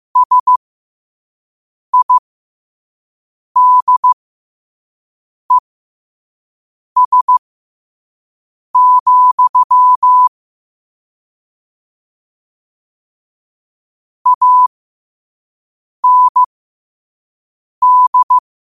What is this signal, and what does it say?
Morse code, heard as a keyed tone, about 15 wpm, spaced as for 5 wpm, 1000 Hz. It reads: SIDES, AND
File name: qotdfiles / 5wpm